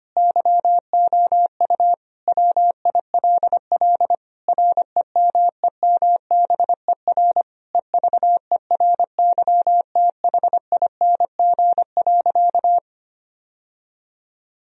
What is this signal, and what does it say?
Morse code, heard as a keyed tone, about 25 wpm, 700 Hz. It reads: YOU WILL REMEMBER EVERYTHING.
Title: Sentences from Top 1000 Words - Review Part 1 - 25wpm